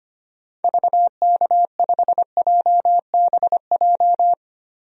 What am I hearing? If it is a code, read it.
Morse code, heard as a keyed tone, about 25 wpm, 700 Hz. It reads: VK5JBJ
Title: DXCC-100 Call Signs - 6 Characters - 25wpm